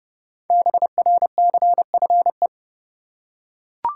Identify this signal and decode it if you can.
Morse code, heard as a keyed tone, about 30 wpm, 700 Hz. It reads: BRCFE
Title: Code Groups - 5 Letters - 30wpm